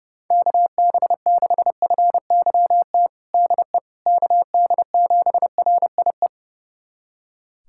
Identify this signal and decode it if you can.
Morse code, heard as a keyed tone, about 32 wpm, 700 Hz. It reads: KB6FYT DE KD7RIE
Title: USA Callsign DE Callsign - 6 Characters - 30wpm